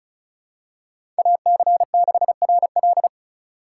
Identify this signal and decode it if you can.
Morse code, heard as a keyed tone, about 36 wpm, 690 Hz. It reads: AC6RL
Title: USA Call Signs - 5 Characters - 3x Character Spacing - 35wpm